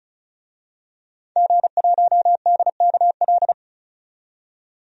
Morse code, heard as a keyed tone, about 36 wpm, 700 Hz. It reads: G1DKL
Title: DXCC-100 Call Signs - 5 Characters - 35wpm